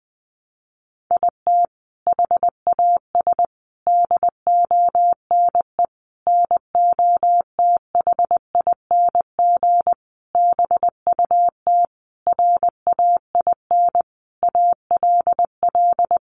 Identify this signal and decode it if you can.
Morse code, heard as a keyed tone, about 20 wpm, 700 Hz. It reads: IT HAS DONE NOTHING BUT RAIN ALL
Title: Sentences from Top 1100 Words 20wpm